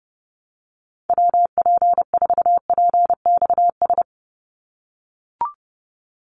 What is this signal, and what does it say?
Morse code, heard as a keyed tone, about 30 wpm, 700 Hz. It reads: WP4PXS
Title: USA Call Signs 30wpm